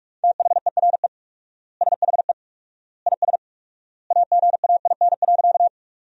Morse code, heard as a keyed tone, about 42 wpm, 700 Hz. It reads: THERE SHE IS AGAIN.
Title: 4x Word-Spacing - Sentences from Top 200 Words - 45wpm